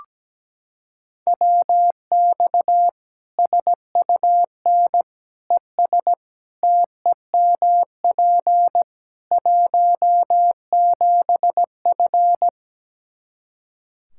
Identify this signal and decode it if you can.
Morse code, heard as a keyed tone, about 17 wpm, 700 Hz. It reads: WX SUN ES TEMP 17F
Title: Practice QSO - Exchange 2 - Info - 17wpm